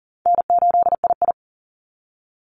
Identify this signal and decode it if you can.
Morse code, heard as a keyed tone, about 40 wpm, 700 Hz. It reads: N8II
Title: Call Signs Repeated 3x - 40wpm